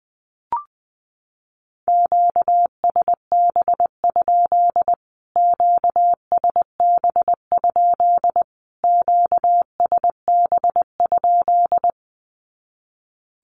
Morse code, heard as a keyed tone, about 20 wpm, 700 Hz. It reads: QSB? QSB? QSB?
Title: Q-Codes Repeated 3x 20wpm